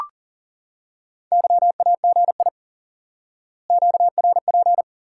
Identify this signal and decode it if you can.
Morse code, heard as a keyed tone, about 40 wpm, 700 Hz. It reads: YAGI QRP
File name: IQE Course 3 - Lesson 89 - YAGI - Two Elements - 40wpm